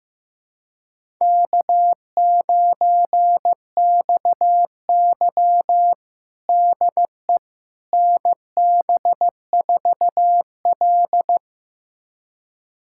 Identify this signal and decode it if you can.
Morse code, heard as a keyed tone, about 15 wpm, 700 Hz. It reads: K9XY DE NB4L
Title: USA Callsign DE Callsign - 4 Characters - 15wpm